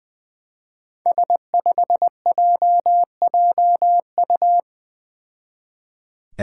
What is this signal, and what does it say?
Morse code, heard as a keyed tone, about 20 wpm, 700 Hz. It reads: S5JJU
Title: Five Challenging Letter-Number Combinations 20wpm